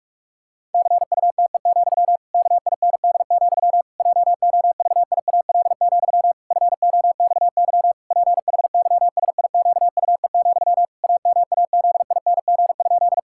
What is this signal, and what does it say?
Morse code, heard as a keyed tone, about 47 wpm, 700 Hz. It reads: CUTE, KIND, JOVIAL, FOXY PHYSIQUE, AMAZING?
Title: Pangram Challenge v2 - 45wpm